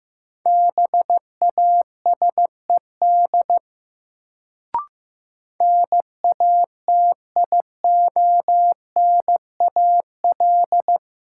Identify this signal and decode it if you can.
Morse code, heard as a keyed tone, about 15 wpm, 700 Hz. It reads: BASED NATIONAL
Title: Sets of 2 Words - New Word Drill - 15wpm